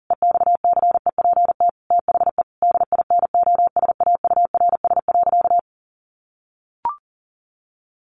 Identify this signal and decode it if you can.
Morse code, heard as a keyed tone, about 40 wpm, 700 Hz. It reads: EXCEPT THE DINOSAURS.
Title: A Sound Of Thunder 40wpm